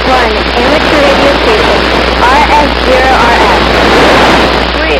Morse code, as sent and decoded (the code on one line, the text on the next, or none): none